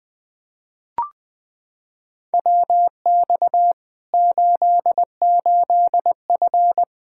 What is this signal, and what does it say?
Morse code, heard as a keyed tone, about 20 wpm, 700 Hz. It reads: WX 88F
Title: Practice QSO - Exchange 2 - Info - 20wpm